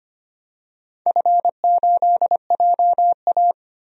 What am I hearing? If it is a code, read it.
Morse code, heard as a keyed tone, about 25 wpm, 700 Hz. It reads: F8JA
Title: DXCC-100 Call Signs - 4 Characters - 25wpm